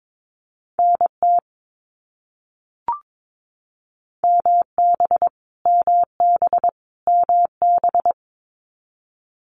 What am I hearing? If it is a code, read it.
Morse code, heard as a keyed tone, about 22 wpm, 700 Hz. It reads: NT MB MB MB
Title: Canadian Provinces and Territories - Repeated 3x - 22wpm